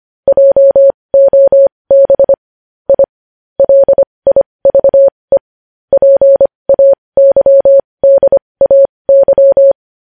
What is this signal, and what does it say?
Morse code, heard as a keyed tone, about 25 wpm, spaced as for 20 wpm, 550 Hz. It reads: JOB I LIVE PAYDAY
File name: podcast / News headlines-1x-25x20